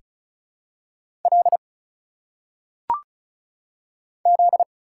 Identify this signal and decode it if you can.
Morse code, heard as a keyed tone, about 34 wpm, 700 Hz. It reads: L Z
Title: Single Letters - Phonetic Pronunciation 35wpm